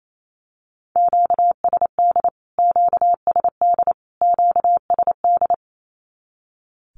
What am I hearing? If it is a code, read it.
Morse code, heard as a keyed tone, about 28 wpm, 700 Hz. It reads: QSD QSD QSD
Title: Q-Codes Repeated 3x 28wpm